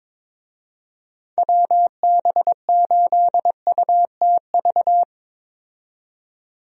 Morse code, heard as a keyed tone, about 22 wpm, 700 Hz. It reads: WB8UTV
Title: USA Call Signs - 6 Characters - 22wpm